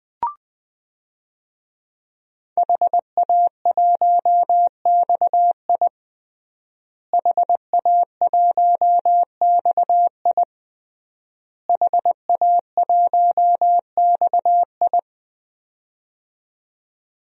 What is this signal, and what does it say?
Morse code, heard as a keyed tone, about 20 wpm, 700 Hz. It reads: HA1XI HA1XI HA1XI
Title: DXCC-100 Call Signs - 5 Characters - Repeated 3x - 20wpm